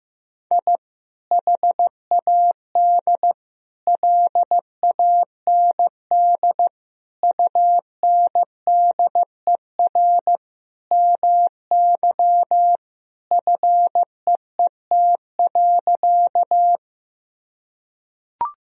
Morse code, heard as a keyed tone, about 16 wpm, 700 Hz. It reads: I HAD LAND UNDER MY FEET.
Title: Sentences from Top 1000 Words - Review Part 1 - 15wpm